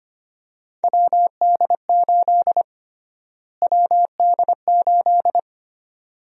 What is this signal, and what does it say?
Morse code, heard as a keyed tone, about 25 wpm, 700 Hz. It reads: WD8 WD8
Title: USA Call Sign Prefixes - Repeated 2x - 25wpm